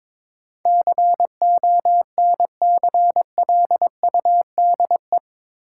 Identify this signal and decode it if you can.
Morse code, heard as a keyed tone, about 21 wpm, 700 Hz. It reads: CONCLUDE
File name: practice / 22wpm / Eight Letter Words - 22wpm